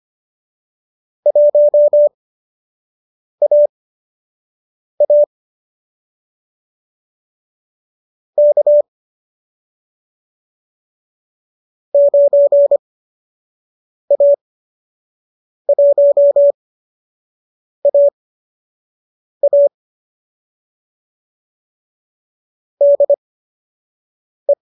Morse code, heard as a keyed tone, about 25 wpm, spaced as for 6 wpm, 570 Hz. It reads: 1AA K 9A1AA DE
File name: qso012 06